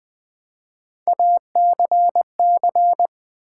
Morse code, heard as a keyed tone, about 20 wpm, 700 Hz. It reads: ACC
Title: Build A Word - 3 Letters At A Time - 6-Letter Words - 20wpm